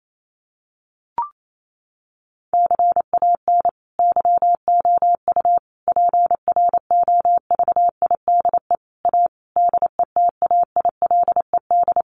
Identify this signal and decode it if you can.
Morse code, heard as a keyed tone, about 28 wpm, 700 Hz. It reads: CAN YOU PROVIDE A DETAILED
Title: Sentences from Top 1500 Words 28wpm